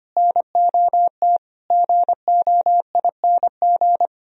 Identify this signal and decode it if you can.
Morse code, heard as a keyed tone, about 32 wpm, 700 Hz. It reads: NOT GOING